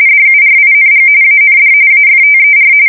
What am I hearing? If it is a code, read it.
RTTY, about 45.45 baud. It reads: DURING AND AFTER
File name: fsk170 / USS-New-Jersey